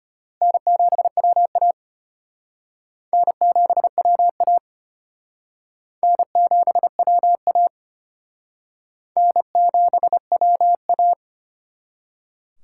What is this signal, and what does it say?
Morse code, heard as a keyed tone, about 32 wpm, 700 Hz. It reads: N7WA N7WA N7WA N7WA